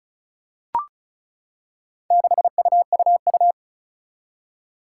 Morse code, heard as a keyed tone, about 36 wpm, 690 Hz. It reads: 6UUU
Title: Four Challenging Letter-Number Combinations 35wpm